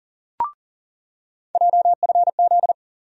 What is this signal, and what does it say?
Morse code, heard as a keyed tone, about 40 wpm, 700 Hz. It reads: JFZ
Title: USA Call Sign Postfixes - 3 Letters 40wpm